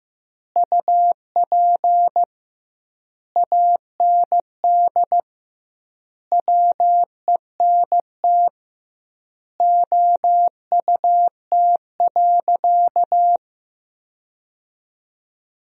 Morse code, heard as a keyed tone, about 16 wpm, 700 Hz. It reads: UP AND WENT OUT.
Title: 2x Word-Spacing - Sentences from Top 300 Words - 15wpm